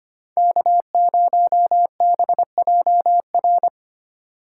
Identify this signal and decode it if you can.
Morse code, heard as a keyed tone, about 25 wpm, 700 Hz. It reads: K0BJR
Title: USA Call Signs - 5 Characters - 2x Character Spacing - 25wpm